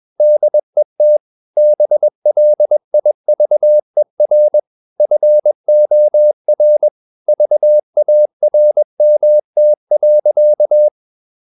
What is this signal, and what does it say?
Morse code, heard as a keyed tone, about 21 wpm, 600 Hz. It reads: DET BLIVER FOR VARMT.
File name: Ferie QRS 21wpm